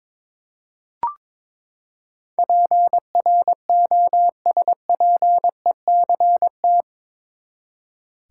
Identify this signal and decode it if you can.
Morse code, heard as a keyed tone, about 23 wpm, 700 Hz. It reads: PROSPECT
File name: practice / 22wpm / Eight Letter Words - 22wpm